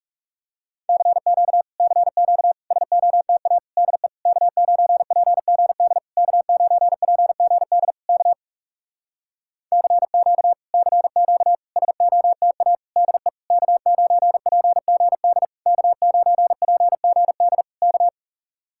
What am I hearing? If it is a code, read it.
Morse code, heard as a keyed tone, about 40 wpm, 700 Hz. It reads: CQ CQ SOTA DE K9PGD K9PGD K CQ CQ SOTA DE K9PGD K9PGD K